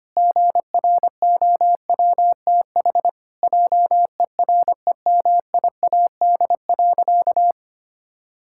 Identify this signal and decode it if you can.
Morse code, heard as a keyed tone, about 25 wpm, 700 Hz. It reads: GROWTH JEREMIAD.
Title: Pangram Challenge v2 - 25wpm